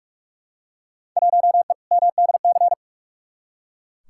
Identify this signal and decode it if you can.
Morse code, heard as a keyed tone, about 41 wpm, 700 Hz. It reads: 1E MDC
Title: Field Day Exchanges - 45wpm